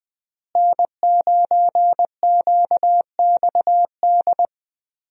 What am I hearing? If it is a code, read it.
Morse code, heard as a keyed tone, about 20 wpm, 700 Hz. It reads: N9QXD
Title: USA Call Signs - 5 Characters - Repeated 2x - 20wpm